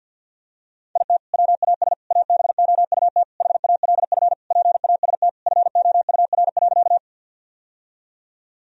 Morse code, heard as a keyed tone, about 50 wpm, 700 Hz. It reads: IT WAS ABOUT HALF PAST FOUR.
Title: Sentences from Top 1000 Words - Review Part 3 - 50wpm